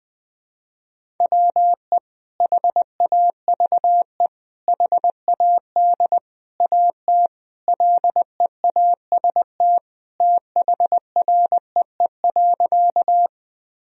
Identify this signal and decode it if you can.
Morse code, heard as a keyed tone, about 21 wpm, 700 Hz. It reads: WE HAVE HAD AT LEAST THREE.
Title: Sentences from Top 1000 Words - Review Part 1 - 20wpm